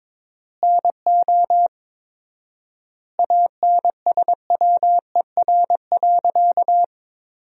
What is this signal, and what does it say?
Morse code, heard as a keyed tone, about 22 wpm, 700 Hz. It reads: NO ANSWER.
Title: 4x Word-Spacing - Sentences from Top 300 Words - 22wpm